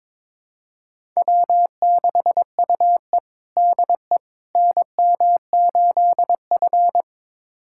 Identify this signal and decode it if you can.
Morse code, heard as a keyed tone, about 22 wpm, 700 Hz. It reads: W6UE DE NM8F
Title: USA Callsign DE Callsign - 4 Characters - 22wpm